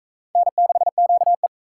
Morse code, heard as a keyed tone, about 37 wpm, 700 Hz. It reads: N6QE